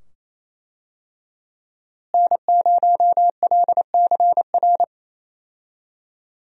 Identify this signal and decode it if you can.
Morse code, heard as a keyed tone, about 28 wpm, 700 Hz. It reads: N0LCR
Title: USA Call Signs - 5 Characters - Repeated 2x - 28wpm